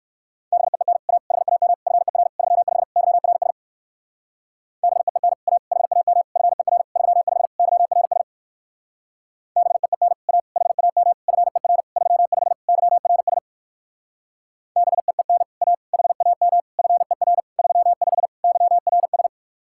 Morse code, heard as a keyed tone, about 58 wpm, 700 Hz. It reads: BEEN A HAM FER 35 YRS BEEN A HAM FER 35 YRS BEEN A HAM FER 35 YRS BEEN A HAM FER 35 YRS